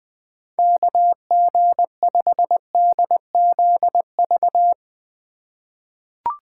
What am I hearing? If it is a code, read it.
Morse code, heard as a keyed tone, about 21 wpm, 700 Hz. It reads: KG5DZV
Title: USA Call Signs - 6 Characters - 2x Character Spacing - 20wpm